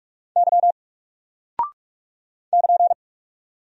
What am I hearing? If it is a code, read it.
Morse code, heard as a keyed tone, about 44 wpm, 700 Hz. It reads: Y <KN>
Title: Single Character (Letter:Number:Punctuation:Prosign) 45wpm